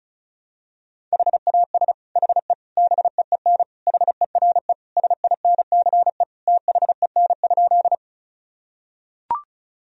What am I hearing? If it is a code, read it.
Morse code, heard as a keyed tone, about 36 wpm, 700 Hz. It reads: HAS HE BEEN HERE SINCE THEN?
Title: Sentences from Top 1000 Words - Review Part 1 - 35wpm